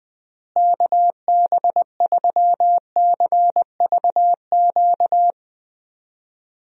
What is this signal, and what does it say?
Morse code, heard as a keyed tone, about 20 wpm, 700 Hz. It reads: KB3CVQ
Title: USA Call Signs - 6 Characters - Repeated 3x - 20wpm